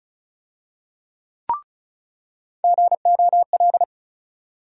Morse code, heard as a keyed tone, about 36 wpm, 700 Hz. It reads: GOL
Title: USA Call Sign Postfixes - 3 Letters 35wpm